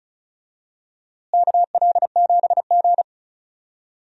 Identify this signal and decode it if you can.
Morse code, heard as a keyed tone, about 36 wpm, 700 Hz. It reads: KL7G
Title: DXCC-100 Call Signs - 4 Characters - Repeated 3x - 35wpm